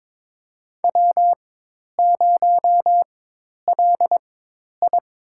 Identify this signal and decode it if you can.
Morse code, heard as a keyed tone, about 22 wpm, spaced as for 10 wpm, 700 Hz. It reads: W0LI
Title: USA Call Signs - 4x Character Spacing - 22wpm